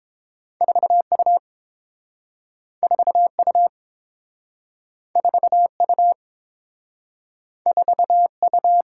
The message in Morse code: 4U 4U 4U 4U

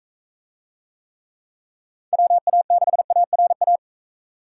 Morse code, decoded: WA6ARA

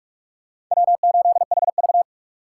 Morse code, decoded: W8HV

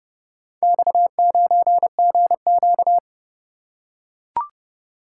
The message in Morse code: X9GQ

